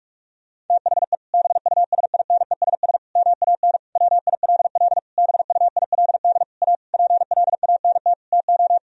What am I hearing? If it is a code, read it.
Morse code, heard as a keyed tone, about 46 wpm, 700 Hz. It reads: THE BUSINESS MAN WILL BUILD A PLANT TO
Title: Sentences from Top 1000 Words - Review Part 1 - 45wpm